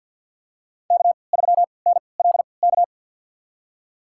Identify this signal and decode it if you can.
Morse code, heard as a keyed tone, about 53 wpm, spaced as for 29 wpm, 700 Hz. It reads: K3NLK
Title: USA Call Signs - 5 Characters - 3x Character Spacing - 50wpm